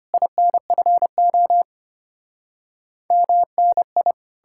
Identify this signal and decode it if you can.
Morse code, heard as a keyed tone, about 33 wpm, 700 Hz. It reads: INFO MNI